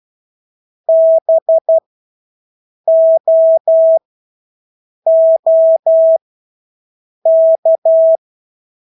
Morse code, heard as a keyed tone, about 12 wpm, spaced as for 6 wpm, 650 Hz. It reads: BOOK
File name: Engelske ord 12-6 wpm